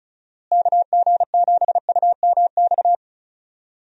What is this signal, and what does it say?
Morse code, heard as a keyed tone, about 36 wpm, 700 Hz. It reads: KG7UMX
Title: USA Call Signs - 6 Characters - Repeated 3x - 35wpm